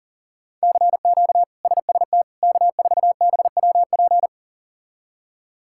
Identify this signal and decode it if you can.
Morse code, heard as a keyed tone, about 40 wpm, 700 Hz. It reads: CQ SST K4BWP
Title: K1USN SST QSO Practice - 40wpm